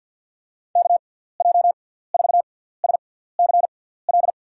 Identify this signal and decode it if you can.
Morse code, heard as a keyed tone, about 52 wpm, spaced as for 17 wpm, 700 Hz. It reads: KJ4SCL
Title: USA Call Signs - 6 Characters - 6x Character Spacing - 50wpm